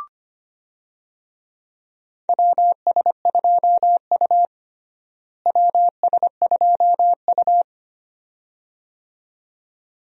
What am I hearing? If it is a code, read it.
Morse code, heard as a keyed tone, about 25 wpm, 700 Hz. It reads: WS2U WS2U
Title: USA Call Signs - 4 Characters - Repeated 2x - 25wpm